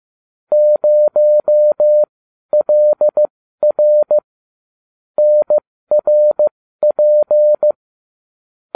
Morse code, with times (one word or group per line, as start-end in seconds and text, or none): 0.5-4.2 0LR
5.2-7.7 NRP